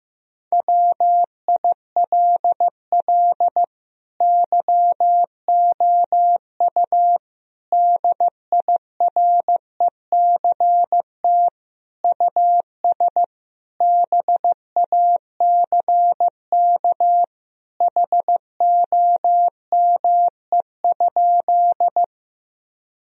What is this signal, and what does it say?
Morse code, heard as a keyed tone, about 15 wpm, 700 Hz. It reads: WILL YOU DIRECT US BACK HOME?
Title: Sentences from Top 1000 Words - Review Part 2 - 15wpm